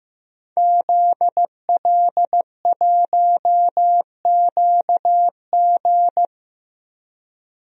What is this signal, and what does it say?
Morse code, heard as a keyed tone, about 15 wpm, 700 Hz. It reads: ZL1QG
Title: DXCC-100 Call Signs - 5 Characters - Repeated 2x - 15wpm